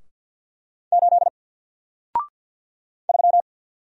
Morse code, heard as a keyed tone, about 48 wpm, 700 Hz. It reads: 8 3
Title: Single Letter-Number Alt 50wpm